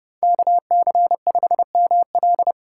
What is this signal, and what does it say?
Morse code, heard as a keyed tone, about 32 wpm, 700 Hz. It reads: KC5ML